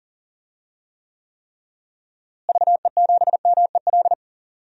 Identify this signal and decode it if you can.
Morse code, heard as a keyed tone, about 40 wpm, 700 Hz. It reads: VE7MEL
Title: DXCC-100 Call Signs - 6 Characters - Repeated 3x - 40wpm